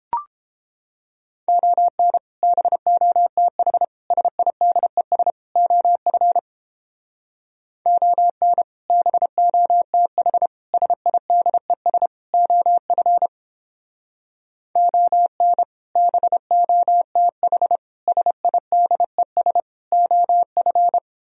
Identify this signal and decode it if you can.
Morse code, heard as a keyed tone, about 29 wpm, 700 Hz. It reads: ON BOTH SIDES OF ON BOTH SIDES OF ON BOTH SIDES OF